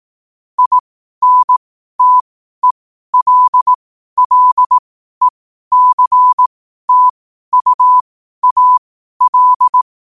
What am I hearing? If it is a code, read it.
Morse code, heard as a keyed tone, about 18 wpm, spaced as for 12 wpm, 1000 Hz. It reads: INTELLECTUAL
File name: qotdfiles / 12wpm